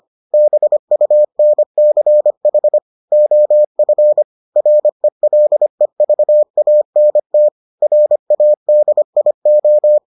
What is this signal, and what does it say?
Morse code, heard as a keyed tone, about 25 wpm, 600 Hz. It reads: BUNCH OF RELEVANT RADIO